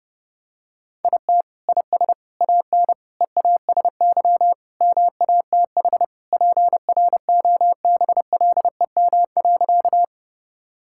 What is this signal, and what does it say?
Morse code, heard as a keyed tone, about 32 wpm, 700 Hz. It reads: IT IS AN EASY MATH PROBLEM.
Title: Sentences from Top 1000 Words - Review Part 6 - 30wpm